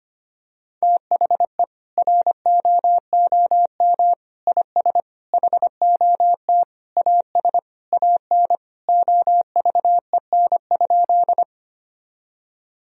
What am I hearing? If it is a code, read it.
Morse code, heard as a keyed tone, about 25 wpm, 700 Hz. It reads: THE ROOM IS HOT AS AN OVEN?
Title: Sentences from Top 1000 Words - Review Part 4 - 25wpm